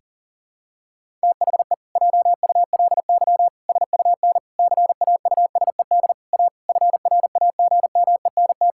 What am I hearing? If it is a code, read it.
Morse code, heard as a keyed tone, about 40 wpm, 700 Hz. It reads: THE JULY SUN CAUSED A FRAGMENT
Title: Pangram Challenge v2 - 40wpm